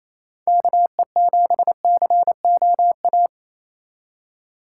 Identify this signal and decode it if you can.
Morse code, heard as a keyed tone, about 28 wpm, 700 Hz. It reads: KE7COA
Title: USA Call Signs - 6 Characters - 3x Character Spacing - 28wpm